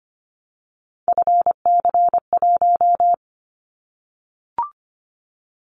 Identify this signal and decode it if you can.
Morse code, heard as a keyed tone, about 25 wpm, 700 Hz. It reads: FC1
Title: Callsign Prefixes for Top 25 DXCC - Repeated 3x - 25wpm